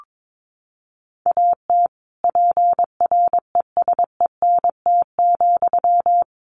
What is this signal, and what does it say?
Morse code, heard as a keyed tone, about 22 wpm, 700 Hz. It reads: AT PRESENT,
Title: Sentences from Top 800 Words 22wpm